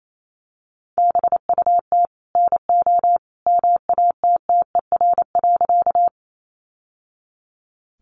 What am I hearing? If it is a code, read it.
Morse code, heard as a keyed tone, about 28 wpm, 700 Hz. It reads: BUT NO MATTER.
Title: Sentences from Top 700 Words 28wpm